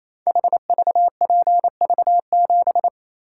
Morse code, heard as a keyed tone, about 27 wpm, 700 Hz. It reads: HVPV7
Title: Code Groups - 5 Letter:Numbers - 28wpm